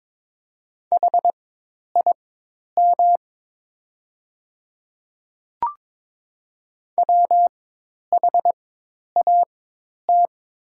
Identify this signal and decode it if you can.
Morse code, heard as a keyed tone, about 22 wpm, spaced as for 10 wpm, 700 Hz. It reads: HIM WHAT
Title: 4x Character Spacing - Top 100 Words - 22wpm